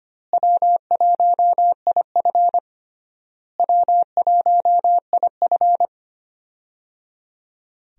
Morse code, heard as a keyed tone, about 25 wpm, 700 Hz. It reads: W1IF W1IF
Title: USA Call Signs - 4 Characters - Repeated 3x - 25wpm